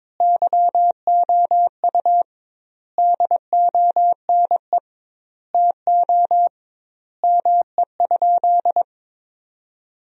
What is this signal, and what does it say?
Morse code, heard as a keyed tone, about 22 wpm, 700 Hz. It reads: YOU DONE TO ME?
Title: 2x Word-Spacing - Sentences from Top 500 Words - 22wpm